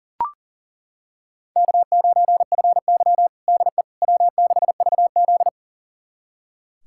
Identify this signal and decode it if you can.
Morse code, heard as a keyed tone, about 40 wpm, 700 Hz. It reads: K9FY DE W6VZ
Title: USA Callsign DE Callsign - 4 Characters - 40wpm